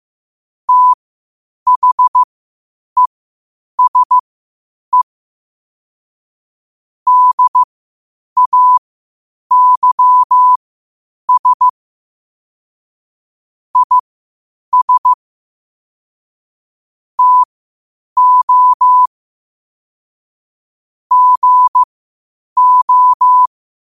Morse code, heard as a keyed tone, about 15 wpm, spaced as for 8 wpm, 1000 Hz. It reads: THESE DAYS IS TO GO
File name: qotdfiles / 8wpm